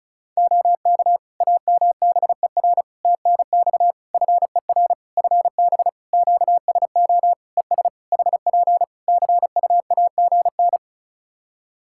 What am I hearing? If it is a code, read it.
Morse code, heard as a keyed tone, about 36 wpm, 700 Hz. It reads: OK AMBER TNX FER FB QSO ES HP CUAGN
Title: Practice QSO - Exchange 4 - Ending - 35wpm